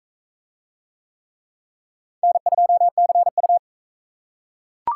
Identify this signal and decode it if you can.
Morse code, heard as a keyed tone, about 44 wpm, 700 Hz. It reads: N2CU